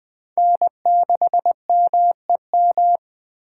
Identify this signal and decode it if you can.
Morse code, heard as a keyed tone, about 24 wpm, 700 Hz. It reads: N6MEM
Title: Speed Racing - USA 1x3 Call Signs 20wpm